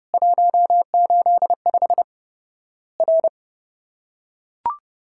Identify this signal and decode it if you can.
Morse code, heard as a keyed tone, about 32 wpm, 700 Hz. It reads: 185 R
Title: Worked All Europe DX Contest - QTC Exchange - 30wpm